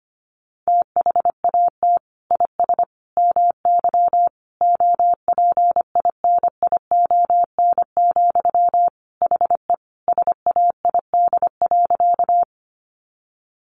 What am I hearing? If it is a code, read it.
Morse code, heard as a keyed tone, about 25 wpm, 700 Hz. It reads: THAT IS MY OPINION, HE SAID.
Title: Sentences from Top 1200 Words 25wpm